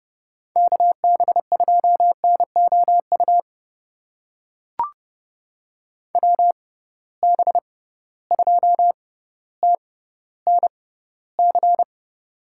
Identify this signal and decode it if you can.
Morse code, heard as a keyed tone, about 32 wpm, spaced as for 16 wpm, 700 Hz. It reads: KB2NOU WB2TNC
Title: USA Call Signs - 6 Characters - 6x Character Spacing - 30wpm